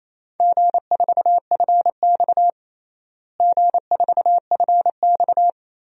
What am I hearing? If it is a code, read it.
Morse code, heard as a keyed tone, about 28 wpm, 700 Hz. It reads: G4FX G4FX
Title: DXCC-100 Call Signs - 4 Characters - Repeated 3x - 28wpm